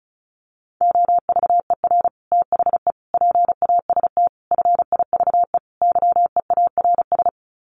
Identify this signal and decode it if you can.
Morse code, heard as a keyed tone, about 34 wpm, 700 Hz. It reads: OVER THE PAST FIVE YEARS
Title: Sets of 5 Words Encore 35wpm